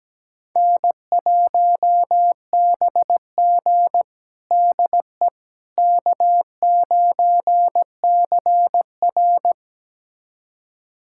Morse code, heard as a keyed tone, about 17 wpm, 700 Hz. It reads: N1BG DE K9CR
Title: USA Callsign DE Callsign - 4 Characters - 17wpm